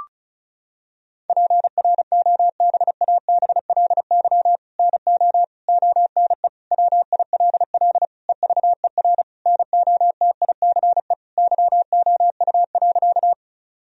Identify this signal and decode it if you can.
Morse code, heard as a keyed tone, about 36 wpm, 700 Hz. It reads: PROBABLY NO ONE WILL EVER NOTICE YOU.
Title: Sentences from Top 1000 Words - Review Part 4 - 35wpm